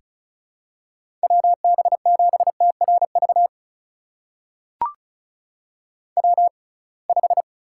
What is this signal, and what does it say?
Morse code, heard as a keyed tone, about 34 wpm, spaced as for 26 wpm, 700 Hz. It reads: WB7TRV W5
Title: USA Call Signs - 6x Character Spacing - 35wpm